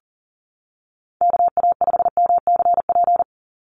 Morse code, heard as a keyed tone, about 40 wpm, 700 Hz. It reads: KA5MCP
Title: USA Call Signs 40wpm